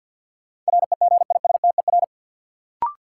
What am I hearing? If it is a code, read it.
Morse code, heard as a keyed tone, about 48 wpm, 700 Hz. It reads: REGISTER
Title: Sentences from Top 1200 Words - New Word Drill - 50wpm